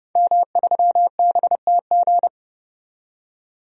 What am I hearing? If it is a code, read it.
Morse code, heard as a keyed tone, about 30 wpm, 700 Hz. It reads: M3BTG
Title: DXCC-100 Call Signs - 5 Characters - 30wpm